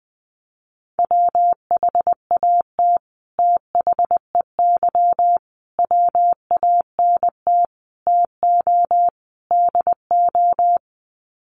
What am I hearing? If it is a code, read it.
Morse code, heard as a keyed tone, about 20 wpm, 700 Hz. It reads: WHAT THEY WANT TO DO
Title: Sets of 5 Words Encore2 20wpm